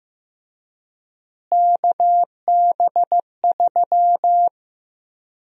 Morse code, heard as a keyed tone, about 15 wpm, 700 Hz. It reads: KB3
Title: USA Call Sign Prefixes - Repeated 2x - 15wpm